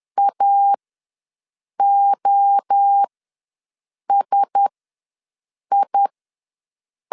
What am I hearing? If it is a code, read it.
Morse code, heard as a keyed tone, about 10 wpm, spaced as for 6 wpm, 800 Hz. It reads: AOSI
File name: LEZIONI / 5 LBVC / 5 LB ws 10wpm 3616